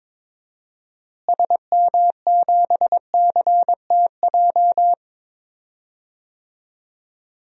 Morse code, heard as a keyed tone, about 21 wpm, 700 Hz. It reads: SM7CTJ
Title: DXCC-100 Call Signs - 6 Characters - Repeated 2x - 22wpm